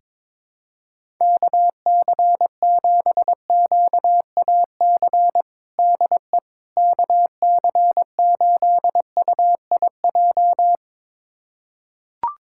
Morse code, heard as a keyed tone, about 21 wpm, 700 Hz. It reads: KC7QAC DE KC8UIJ
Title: USA Callsign DE Callsign - 6 Characters - 22wpm